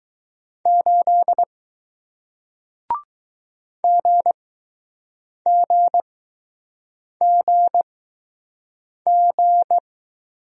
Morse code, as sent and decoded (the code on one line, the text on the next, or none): ---.. / --. --. --. --.
8 GGGG